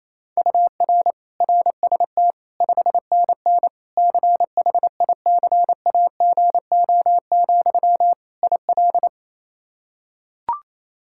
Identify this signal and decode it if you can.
Morse code, heard as a keyed tone, about 28 wpm, 700 Hz. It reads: UR RST 5NN CHICAGO, IL
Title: K1USN SST QSO Practice 28wpm